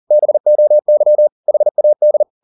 Morse code, heard as a keyed tone, about 40 wpm, 600 Hz. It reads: BOY HAD
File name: the old man and the sea 1 40wpm